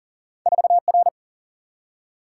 Morse code, heard as a keyed tone, about 40 wpm, 700 Hz. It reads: 4R
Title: Single-Letter Single-Number 40wpm